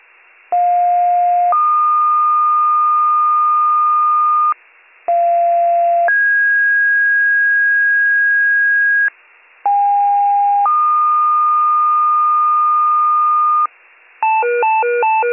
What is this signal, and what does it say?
Two-tone page, heard as a one-second tone, then a three-second tone, about 710 then 1220 Hz, about 690 then 1650 Hz, about 810 then 1220 Hz. It is heard from 0.5 to 13.7 s.